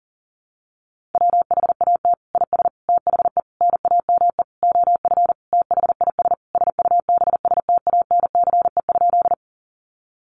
Morse code, heard as a keyed tone, about 40 wpm, 700 Hz. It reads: WHAT IS THE NAME OF THIS SUBSTANCE?
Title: Sentences from Top 1500 Words 40wpm